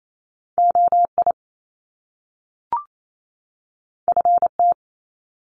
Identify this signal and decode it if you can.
Morse code, heard as a keyed tone, about 28 wpm, 700 Hz. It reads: OI FT
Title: Top Two Letter Combinations in Words 28wpm